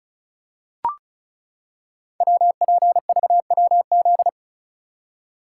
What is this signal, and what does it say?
Morse code, heard as a keyed tone, about 36 wpm, 700 Hz. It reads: WPVWZ
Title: Code Groups - 5 Letters - 35wpm